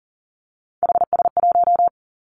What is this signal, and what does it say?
Morse code, heard as a keyed tone, about 42 wpm, 700 Hz. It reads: HS1
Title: Three Challenging Letter-Number Combinations 40wpm